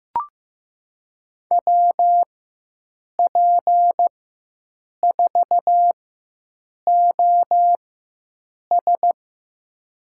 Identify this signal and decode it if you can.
Morse code, heard as a keyed tone, about 15 wpm, spaced as for 7 wpm, 700 Hz. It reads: WP4OS
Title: USA Call Signs - 5 Characters - 4x Character Spacing - 15wpm